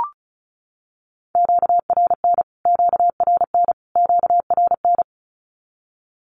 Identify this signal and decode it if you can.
Morse code, heard as a keyed tone, about 34 wpm, 700 Hz. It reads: QRN QRN QRN
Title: Q-Codes Repeated 3x 35wpm